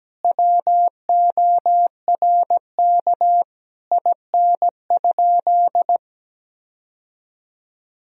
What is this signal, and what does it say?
Morse code, heard as a keyed tone, about 17 wpm, 700 Hz. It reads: WORK IN?
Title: Sentences from Top 1000 Words - Review Part 4 - 17wpm